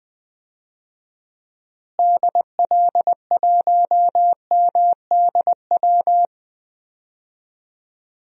Morse code, heard as a keyed tone, about 20 wpm, 700 Hz. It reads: DL1MDW
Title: DXCC-100 Call Signs - 6 Characters - Repeated 2x - 20wpm